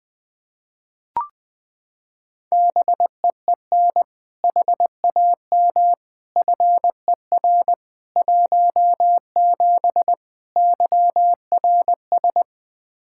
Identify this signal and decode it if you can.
Morse code, heard as a keyed tone, about 20 wpm, 700 Hz. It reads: BEEN HAM FER 17 YRS
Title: Practice QSO - Exchange 3 - More Info - 20wpm